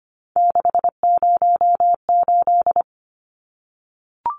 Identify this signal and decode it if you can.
Morse code, heard as a keyed tone, about 25 wpm, 700 Hz. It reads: 608
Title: Numbers - 3-Digit 25wpm